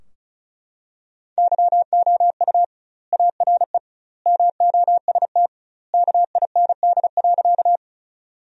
Morse code, heard as a keyed tone, about 36 wpm, 700 Hz. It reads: YOU ARE MOST KIND.
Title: 2x Word-Spacing - Sentences from Top 200 Words - 35wpm